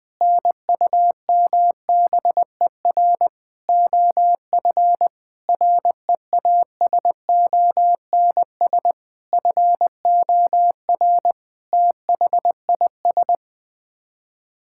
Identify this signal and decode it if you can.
Morse code, heard as a keyed tone, about 21 wpm, 700 Hz. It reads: NUMBER OF REASONS FOR THIS
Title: Sets of 8 Words - 20wpm